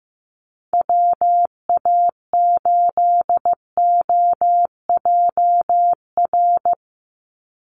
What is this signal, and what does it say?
Morse code, heard as a keyed tone, about 15 wpm, 700 Hz. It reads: WA8OJR
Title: Call Signs Repeated 2x - 15wpm